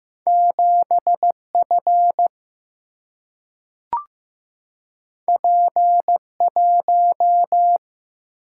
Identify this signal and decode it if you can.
Morse code, heard as a keyed tone, about 15 wpm, 700 Hz. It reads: 7F P1
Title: Single-Letter Single-Number 15wpm